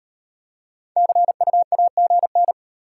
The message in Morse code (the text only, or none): CU AGN